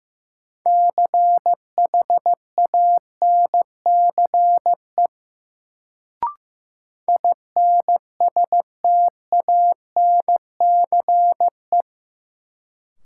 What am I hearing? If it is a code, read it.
Morse code, heard as a keyed tone, about 15 wpm, 700 Hz. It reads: CHANCE INSTANCE
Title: Sentences from Top 800 Words - New Word Drill - 15wpm